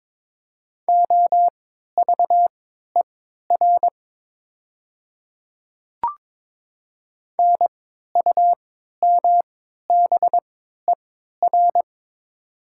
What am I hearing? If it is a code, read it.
Morse code, heard as a keyed tone, about 22 wpm, spaced as for 12 wpm, 700 Hz. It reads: OVER NUMBER
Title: 3x Character Spacing - Top 200 Words - 22wpm